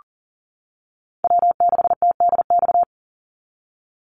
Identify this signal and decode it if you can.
Morse code, heard as a keyed tone, about 40 wpm, 700 Hz. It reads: W6TDX
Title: NA SOTA Chaser Callsigns 40wpm